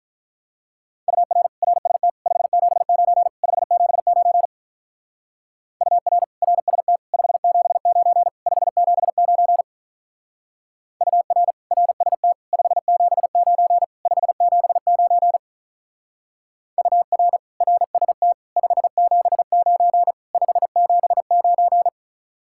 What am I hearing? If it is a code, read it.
Morse code, heard as a keyed tone, about 44 wpm, 700 Hz. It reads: UR RST 579 579 UR RST 579 579 UR RST 579 579 UR RST 579 579